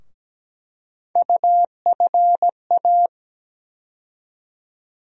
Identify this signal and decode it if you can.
Morse code, heard as a keyed tone, about 17 wpm, 700 Hz. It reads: UFA
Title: USA Call Sign Postfixes - 3 Letters 17wpm